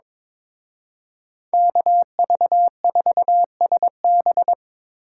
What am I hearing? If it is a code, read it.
Morse code, heard as a keyed tone, about 24 wpm, 700 Hz. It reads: KV4SB
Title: Speed Racing - USA 2x2 Call Signs 22wpm